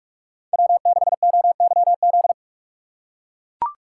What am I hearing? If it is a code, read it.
Morse code, heard as a keyed tone, about 44 wpm, 700 Hz. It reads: W6OYZ